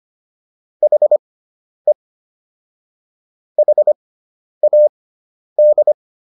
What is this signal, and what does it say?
Morse code, heard as a keyed tone, about 25 wpm, spaced as for 10 wpm, 600 Hz. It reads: HE HAD